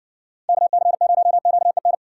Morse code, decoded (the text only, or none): KC0QEN